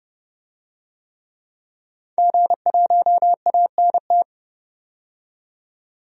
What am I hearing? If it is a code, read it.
Morse code, heard as a keyed tone, about 30 wpm, 700 Hz. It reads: G1ANT
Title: DXCC-100 Call Signs - 5 Characters - 30wpm